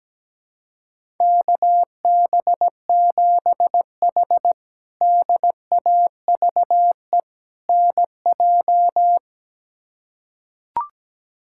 Morse code, heard as a keyed tone, about 17 wpm, 700 Hz. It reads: KB7H DAVE NJ
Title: K1USN SST QSO Practice - 17wpm